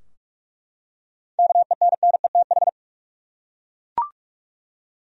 Morse code, heard as a keyed tone, about 47 wpm, 700 Hz. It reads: KENNETH
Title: USA Common Names - 45wpm